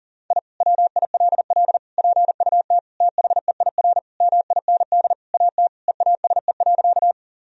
Morse code, heard as a keyed tone, about 40 wpm, 700 Hz. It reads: I WILL PUT THEIR MIND AT EASE.
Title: Sentences from Top 1000 Words - Review Part 1 - 40wpm